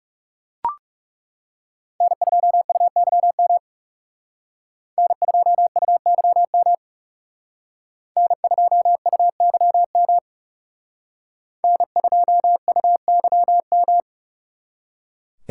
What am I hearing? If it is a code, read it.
Morse code, heard as a keyed tone, about 38 wpm, 700 Hz. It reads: N2UYM N2UYM N2UYM N2UYM